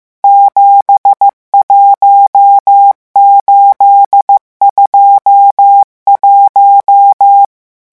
Morse code, heard as a keyed tone, about 15 wpm, 800 Hz. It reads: 71821
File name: morse mp3 / revision11 chiffres vitesse 14